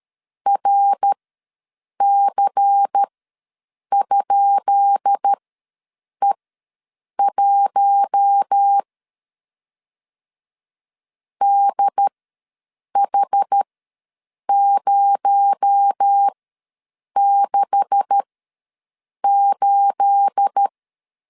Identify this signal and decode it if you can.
Morse code, heard as a keyed tone, about 13 wpm, spaced as for 7 wpm, 800 Hz. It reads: RC?E1 DH068
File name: LEZIONI / 10 SIGNS / 10B INT ws 12wpm 3527